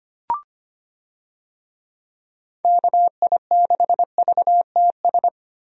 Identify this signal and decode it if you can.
Morse code, heard as a keyed tone, about 25 wpm, 700 Hz. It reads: KI6VTS
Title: USA Call Signs - 6 Characters - Repeated 2x - 25wpm